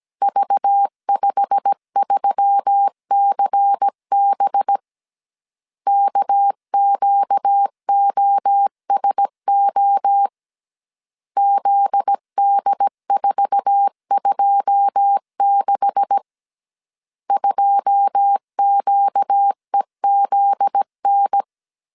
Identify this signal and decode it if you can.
Morse code, heard as a keyed tone, about 17 wpm, 800 Hz. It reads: V53CB KQOSO ZD426 2QEZN
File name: CW / DET / 1 9 1500C S A 16wpm 2828